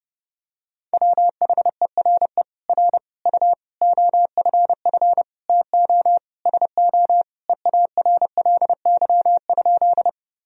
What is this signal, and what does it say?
Morse code, heard as a keyed tone, about 30 wpm, 700 Hz. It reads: WHERE R U OFF TO SO EARLY?
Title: Sentences from Top 400 Words and Abbreviations 30wpm